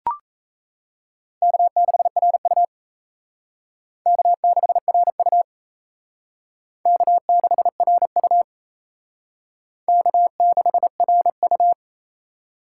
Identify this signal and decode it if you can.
Morse code, heard as a keyed tone, about 34 wpm, 700 Hz. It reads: K6RU K6RU K6RU K6RU
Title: Speed Racing - USA 1x2 Call Signs 28wpm